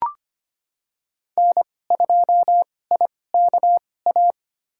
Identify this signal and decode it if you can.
Morse code, heard as a keyed tone, about 25 wpm, spaced as for 18 wpm, 700 Hz. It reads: N2IKA
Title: USA Call Signs - 5 Characters - 2x Character Spacing - 25wpm